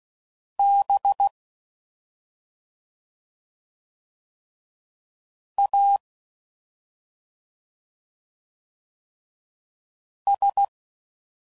Morse code, heard as a keyed tone, about 16 wpm, 800 Hz. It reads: B A S